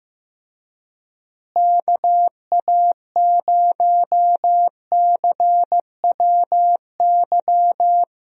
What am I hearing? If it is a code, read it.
Morse code, heard as a keyed tone, about 16 wpm, 700 Hz. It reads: KA0CWY
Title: USA Call Signs - 6 Characters - Repeated 3x - 15wpm